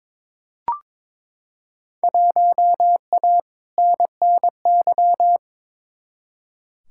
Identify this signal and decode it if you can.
Morse code, heard as a keyed tone, about 22 wpm, 700 Hz. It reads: 1A NNY
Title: Field Day Exchanges - 22wpm